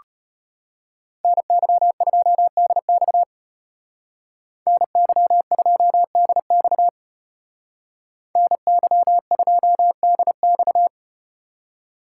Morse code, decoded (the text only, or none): NY2DX NY2DX NY2DX